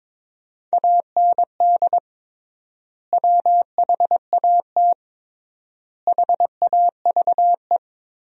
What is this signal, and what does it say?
Morse code, heard as a keyed tone, about 22 wpm, 700 Hz. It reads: AND WHAT HAVE
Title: 3x Word-Spacing - Sentences from Top 500 Words - 22wpm